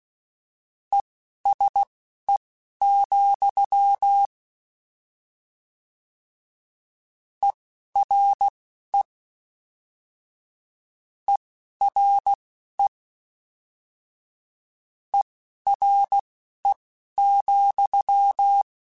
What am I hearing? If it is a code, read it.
Morse code, heard as a keyed tone, about 16 wpm, spaced as for 12 wpm, 800 Hz. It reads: ESE, ERE ERE ERE,